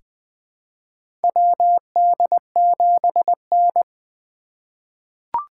Morse code, heard as a keyed tone, about 20 wpm, 700 Hz. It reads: WD7N
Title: USA Call Signs - 4 Characters - 4x Character Spacing - 20wpm